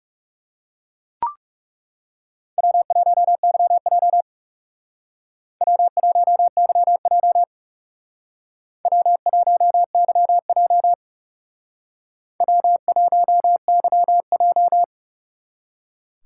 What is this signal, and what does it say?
Morse code, heard as a keyed tone, about 36 wpm, 700 Hz. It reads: W1YJ W1YJ W1YJ W1YJ
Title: Speed Racing - USA Call Signs 30wpm